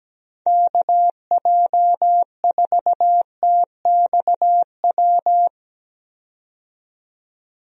Morse code, heard as a keyed tone, about 17 wpm, 700 Hz. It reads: KJ4TXW
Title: USA Call Signs - 6 Characters - 17wpm